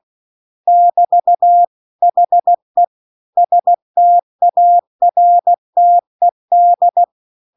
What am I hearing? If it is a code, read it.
Morse code, heard as a keyed tone, about 16 wpm, 700 Hz. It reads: <BT> HE STARTED